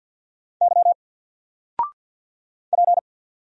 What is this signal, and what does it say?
Morse code, heard as a keyed tone, about 50 wpm, 700 Hz. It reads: Y P
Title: Single Character (Letter:Number:Punctuation:Prosign) 50wpm